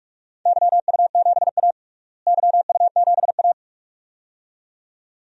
Morse code, heard as a keyed tone, about 41 wpm, 700 Hz. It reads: YU7A YU7A
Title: DXCC-100 Call Signs - 4 Characters - Repeated 2x - 45wpm